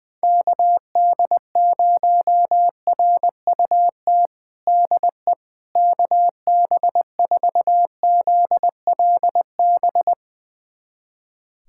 Morse code, heard as a keyed tone, about 20 wpm, 700 Hz. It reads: KD0RUT DE KB4ZLB
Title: USA Callsign DE Callsign - 6 Characters - 20wpm